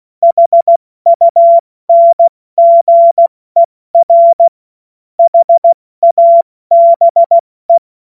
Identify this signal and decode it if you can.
Morse code, heard as a keyed tone, about 16 wpm, spaced as for 14 wpm, 680 Hz. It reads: HUNGER HABE
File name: Witz-15-Juli-2025-70-F80